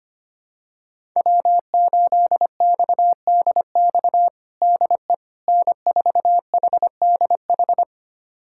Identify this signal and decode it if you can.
Morse code, heard as a keyed tone, about 25 wpm, 700 Hz. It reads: W8XDX DE N4HDH
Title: USA Callsign DE Callsign - 5 Characters - 25wpm